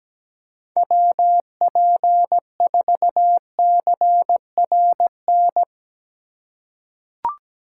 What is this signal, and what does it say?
Morse code, heard as a keyed tone, about 17 wpm, 700 Hz. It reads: WP4CRN
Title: USA Call Signs - 6 Characters - 3x Character Spacing - 17wpm